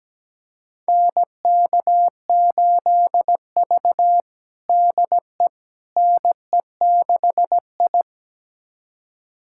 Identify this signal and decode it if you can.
Morse code, heard as a keyed tone, about 17 wpm, 700 Hz. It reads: NK8V DE NE6I
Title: USA Callsign DE Callsign - 4 Characters - 17wpm